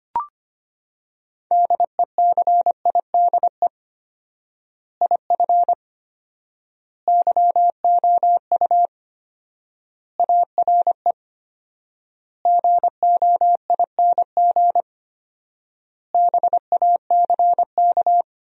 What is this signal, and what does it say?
Morse code, heard as a keyed tone, about 25 wpm, 700 Hz. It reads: DECIDE IF YOU ARE GOING BACK
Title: 4x Word-Spacing - Sentences from Top 500 Words - 25wpm